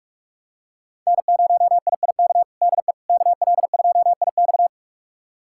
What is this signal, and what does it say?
Morse code, heard as a keyed tone, about 46 wpm, 700 Hz. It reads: N0IIK DE KL2IX
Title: USA Callsign DE Callsign - 5 Characters - 45wpm